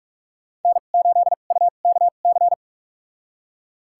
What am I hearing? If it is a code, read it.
Morse code, heard as a keyed tone, about 42 wpm, spaced as for 31 wpm, 700 Hz. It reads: N8UKC
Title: USA Call Signs - 5 Characters - 2x Character Spacing - 45wpm